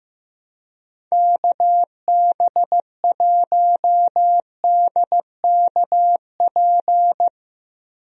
Morse code, heard as a keyed tone, about 15 wpm, 700 Hz. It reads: KB1DKP
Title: USA Call Signs - 6 Characters - 15wpm